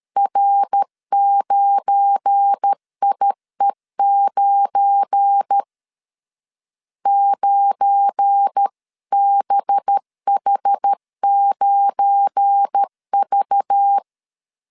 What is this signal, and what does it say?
Morse code, heard as a keyed tone, about 13 wpm, 800 Hz. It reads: R9IE9 9BH9V
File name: LEZIONI / 7 GF9Z / 7 GF9 12wpm 2319